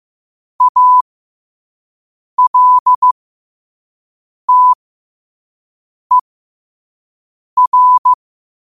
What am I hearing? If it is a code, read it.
Morse code, heard as a keyed tone, about 15 wpm, spaced as for 5 wpm, 1000 Hz. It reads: ALTER
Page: New quotes every day in morse code at 5 Words per minute.